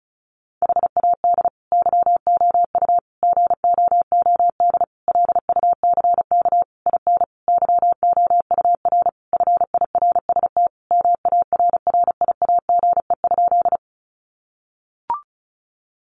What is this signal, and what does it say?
Morse code, heard as a keyed tone, about 34 wpm, 700 Hz. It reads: HAD YOU GOOD LUCK IN YOUR FIRST MARRIAGE?
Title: Sentences from Top 1500 Words 35wpm